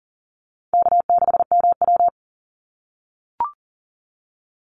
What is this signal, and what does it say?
Morse code, heard as a keyed tone, about 40 wpm, 700 Hz. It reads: K6MW
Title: NA SOTA Chaser Callsigns 40wpm